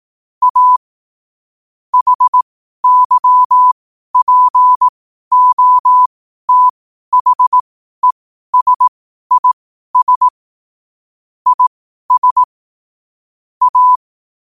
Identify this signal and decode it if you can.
Morse code, heard as a keyed tone, about 18 wpm, spaced as for 12 wpm, 1000 Hz. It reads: A HYPOTHESIS IS A
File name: qotdfiles / 12wpm